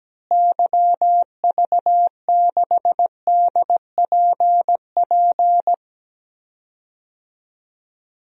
Morse code, heard as a keyed tone, about 17 wpm, 700 Hz. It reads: YV6DPP
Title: DXCC-100 Call Signs - 6 Characters - Repeated 2x - 17wpm